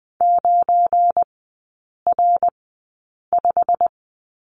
Morse code, heard as a keyed tone, about 20 wpm, spaced as for 8 wpm, 700 Hz. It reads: 9R5
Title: 184 Lesson 30e - Letter b - Three Characters 20wpm